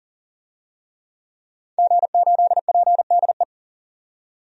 Morse code, heard as a keyed tone, about 39 wpm, 700 Hz. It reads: G8PDE
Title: DXCC-100 Call Signs - 5 Characters - 40wpm